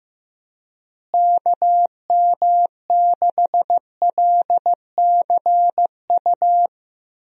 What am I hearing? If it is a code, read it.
Morse code, heard as a keyed tone, about 15 wpm, 700 Hz. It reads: KM6LCU
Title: USA Call Signs - 6 Characters - Repeated 3x - 15wpm